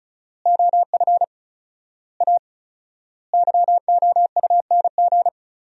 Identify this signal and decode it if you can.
Morse code, heard as a keyed tone, about 36 wpm, 700 Hz. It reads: OF A YOUNG
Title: 4x Word-Spacing - Sentences from Top 500 Words - 35wpm